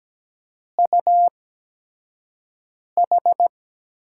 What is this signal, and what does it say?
Morse code, heard as a keyed tone, about 17 wpm, spaced as for 5 wpm, 700 Hz. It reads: UH
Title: USA Call Signs - 6 Characters - 8x Character Spacing - 17wpm